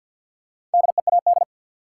Morse code, heard as a keyed tone, about 47 wpm, 700 Hz. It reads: DEAD
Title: Four Letter Words - 50wpm